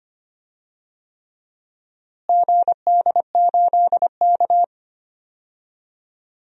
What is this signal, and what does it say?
Morse code, heard as a keyed tone, about 25 wpm, 700 Hz. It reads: GD8K
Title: DXCC-100 Call Signs - 4 Characters - Repeated 3x - 25wpm